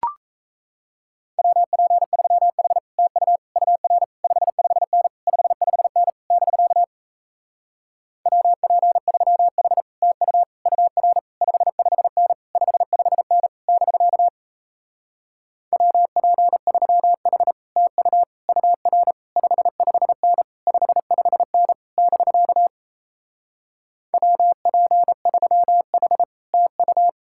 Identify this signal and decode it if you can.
Morse code, heard as a keyed tone, about 38 wpm, 700 Hz. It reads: WP3H TU UR 55N 55N <BK> WP3H TU UR 55N 55N <BK> WP3H TU UR 55N 55N <BK> WP3H TU